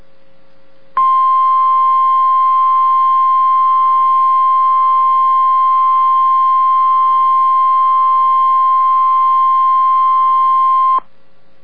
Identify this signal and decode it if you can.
Emergency Alert System (SAME) tones (the 1050 Hz attention tone). It is heard from 0.9 to 11.0 s.